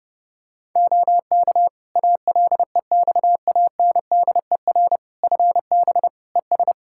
Morse code, heard as a keyed tone, about 30 wpm, 700 Hz. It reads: OK ALEXANDER FB ES
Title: Practice QSO - Exchange 2 - Info - 30wpm